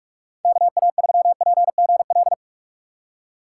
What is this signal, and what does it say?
Morse code, heard as a keyed tone, about 41 wpm, 700 Hz. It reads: KA3PGL
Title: USA Call Signs - 6 Characters - 3x Character Spacing - 45wpm